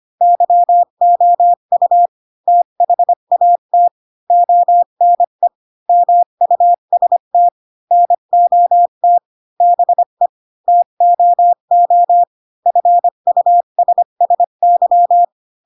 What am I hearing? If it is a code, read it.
Morse code, heard as a keyed tone, about 25 wpm, 700 Hz. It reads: YOU THAT ONE MUST NOT BE TOO FUSSY
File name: Pinocchio / Pinocchio ch7 20 25 ik1hge